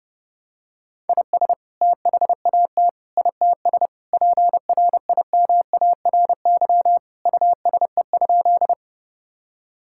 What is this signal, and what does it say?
Morse code, heard as a keyed tone, about 30 wpm, 700 Hz. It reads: IS THAT ITS PRIMARY USE?
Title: Sentences from Top 1000 Words - Review Part 5 - 30wpm